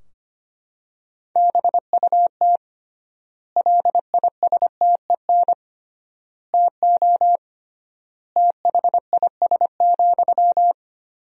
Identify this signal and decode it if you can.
Morse code, heard as a keyed tone, about 25 wpm, 700 Hz. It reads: BUT LISTEN TO THIS,
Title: 3x Word-Spacing - Sentences from Top 400 Words - 25wpm